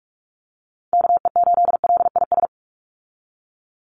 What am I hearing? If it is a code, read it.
Morse code, heard as a keyed tone, about 44 wpm, 700 Hz. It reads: KE8LIS
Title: USA Call Signs 45wpm